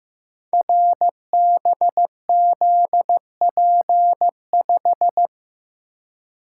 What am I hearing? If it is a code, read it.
Morse code, heard as a keyed tone, about 15 wpm, 700 Hz. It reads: RBZP5
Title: Code Groups - 5 Letter:Numbers - 15wpm